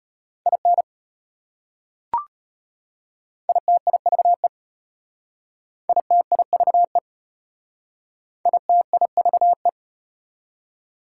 Morse code, heard as a keyed tone, about 33 wpm, 690 Hz. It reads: IN ITIVE ITIVE ITIVE